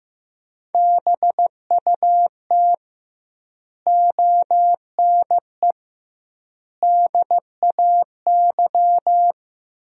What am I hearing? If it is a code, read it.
Morse code, heard as a keyed tone, about 15 wpm, 700 Hz. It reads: BUT ONE DAY
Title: 2x Word-Spacing - Sentences from Top 400 Words - 15wpm